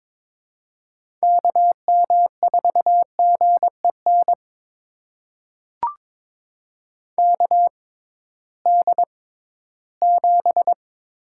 Morse code, heard as a keyed tone, about 22 wpm, spaced as for 14 wpm, 700 Hz. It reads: KM4GEN KD7
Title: USA Call Signs - 6 Characters - 6x Character Spacing - 22wpm